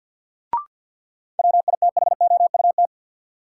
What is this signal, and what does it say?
Morse code, heard as a keyed tone, about 52 wpm, 700 Hz. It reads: WITHOUT
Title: Sentences from Top 600 Words - New Word Drill - 50wpm